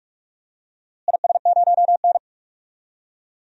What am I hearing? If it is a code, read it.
Morse code, heard as a keyed tone, about 46 wpm, 700 Hz. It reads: IS0N
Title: DXCC-100 Call Signs - 4 Characters - Repeated 2x - 45wpm